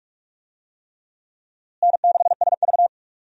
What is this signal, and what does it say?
Morse code, heard as a keyed tone, about 44 wpm, 700 Hz. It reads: N6SV
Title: USA Call Signs - 4 Characters - Repeated 3x - 45wpm